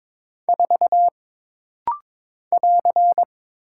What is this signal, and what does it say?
Morse code, heard as a keyed tone, about 22 wpm, 700 Hz. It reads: 4 <AR>
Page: This rapid-fire timing is geared to push your ICR (Instant Character Recognition) skills.